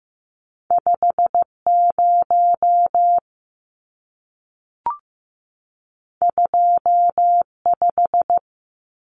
Morse code, heard as a keyed tone, about 15 wpm, 700 Hz. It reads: 50 25
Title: Numbers - 2-Digit 15wpm